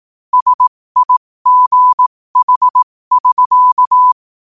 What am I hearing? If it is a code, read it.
Morse code, heard as a keyed tone, about 18 wpm, spaced as for 15 wpm, 1000 Hz. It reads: SIGH<SK>
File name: qotdfiles / 15wpm